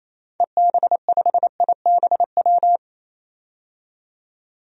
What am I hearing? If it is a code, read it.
Morse code, heard as a keyed tone, about 28 wpm, 700 Hz. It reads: EB5IBW
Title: DXCC-100 Call Signs - 6 Characters - 28wpm